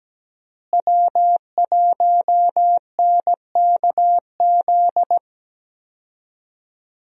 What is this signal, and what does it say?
Morse code, heard as a keyed tone, about 17 wpm, 700 Hz. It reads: W1NKZ
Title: USA Call Signs - 5 Characters - Repeated 3x - 17wpm